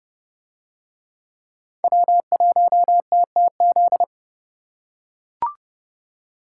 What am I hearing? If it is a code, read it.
Morse code, heard as a keyed tone, about 30 wpm, 700 Hz. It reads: W1TTZ